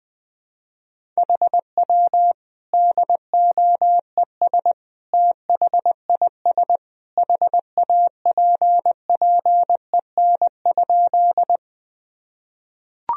Morse code, heard as a keyed tone, about 21 wpm, 700 Hz. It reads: HW DOES THIS HAPPEN?
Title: Sentences from Top 400 Words and Abbreviations 20wpm